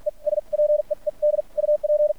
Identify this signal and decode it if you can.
Morse code, heard as a keyed tone, about 56 wpm, spaced as for 42 wpm, 600 Hz. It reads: E SPEED UP
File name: archived / CRSnet